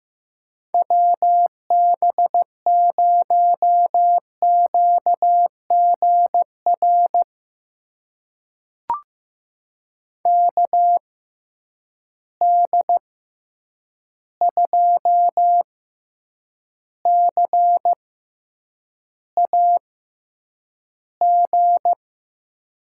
Morse code, heard as a keyed tone, about 15 wpm, spaced as for 8 wpm, 700 Hz. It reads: WB0QGR KD2CAG
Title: USA Call Signs - 6x Character Spacing - 15wpm